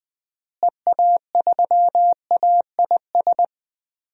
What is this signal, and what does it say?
Morse code, heard as a keyed tone, about 20 wpm, 700 Hz. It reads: EA3AIS
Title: DXCC-100 Call Signs - 6 Characters - Repeated 2x - 20wpm